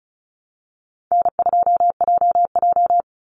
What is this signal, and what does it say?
Morse code, heard as a keyed tone, about 34 wpm, 700 Hz. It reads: N2JJ
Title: Call Signs - 35wpm